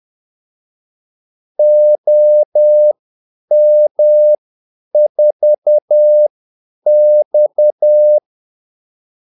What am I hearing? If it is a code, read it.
Morse code, heard as a keyed tone, about 10 wpm, spaced as for 8 wpm, 600 Hz. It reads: OM4X
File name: Callsigns-10-8wpm